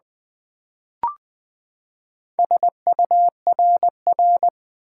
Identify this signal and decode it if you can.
Morse code, heard as a keyed tone, about 20 wpm, 700 Hz. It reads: SURR
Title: Build A Word - 4 Letters At A Time - 11-Letter Words - 20wpm